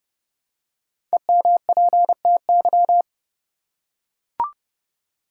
Morse code, heard as a keyed tone, about 30 wpm, 700 Hz. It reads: EMPTY
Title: Five Letter Words - 30wpm